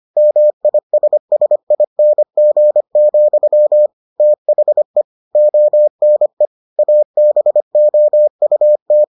Morse code, heard as a keyed tone, about 25 wpm, 600 Hz. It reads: MISSING, THE ONE ABOUT